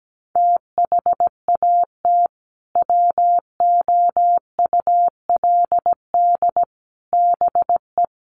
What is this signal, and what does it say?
Morse code, heard as a keyed tone, about 17 wpm, 700 Hz. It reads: THAT WOULD BE
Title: Sets of 3 Words 17wpm